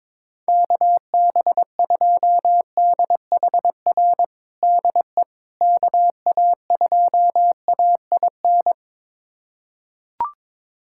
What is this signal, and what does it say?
Morse code, heard as a keyed tone, about 22 wpm, 700 Hz. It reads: KB2DHR DE KA2AIN
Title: USA Callsign DE Callsign - 6 Characters - 22wpm